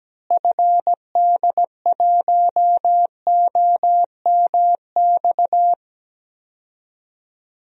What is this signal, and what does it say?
Morse code, heard as a keyed tone, about 17 wpm, 700 Hz. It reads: FD1OMX
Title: DXCC-100 Call Signs - 6 Characters - Repeated 2x - 17wpm